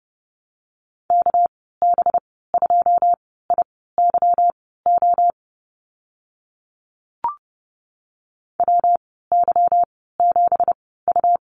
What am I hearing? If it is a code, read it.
Morse code, heard as a keyed tone, about 30 wpm, spaced as for 17 wpm, 700 Hz. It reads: KB2IYO WY7U
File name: Lesson 37g - Letter x - Call Signs 30wpm